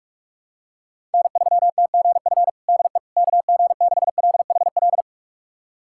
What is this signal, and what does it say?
Morse code, heard as a keyed tone, about 42 wpm, 700 Hz. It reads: N3TGF DE KG6LHL
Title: USA Callsign DE Callsign 45wpm